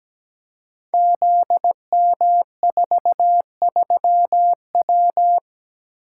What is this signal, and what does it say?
Morse code, heard as a keyed tone, about 17 wpm, 700 Hz. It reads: ZM43W
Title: Code Groups - 5 Letter:Numbers - 17wpm